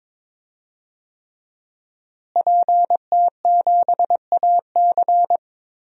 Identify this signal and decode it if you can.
Morse code, heard as a keyed tone, about 22 wpm, 700 Hz. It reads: PT7AC
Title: DXCC-100 Call Signs - 5 Characters - Repeated 3x - 22wpm